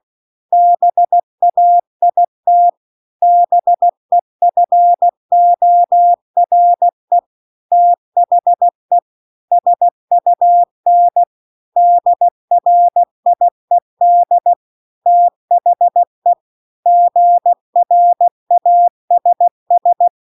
Morse code, heard as a keyed tone, about 16 wpm, 700 Hz. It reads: BAIT BEFORE THE SUN DRIED THE GRASS
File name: CW-The-Big-Twoheaded-River-2-16WPM